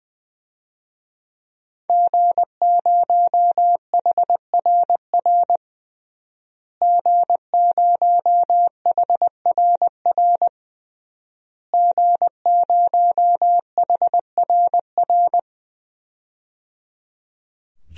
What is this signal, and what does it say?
Morse code, heard as a keyed tone, about 20 wpm, 700 Hz. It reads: G0HRR G0HRR G0HRR
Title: DXCC-100 Call Signs - 5 Characters - Repeated 3x - 20wpm